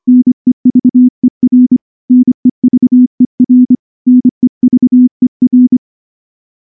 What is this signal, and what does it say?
Morse code, heard as a keyed tone, about 24 wpm, 250 Hz. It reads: NEVER NEVER NEVER
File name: morse / morse-training-25WPM